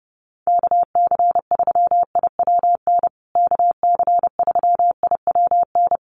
Morse code, heard as a keyed tone, about 30 wpm, 700 Hz. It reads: KC3IWN KC3IWN
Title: USA Call Signs Repeated 2x 30wpm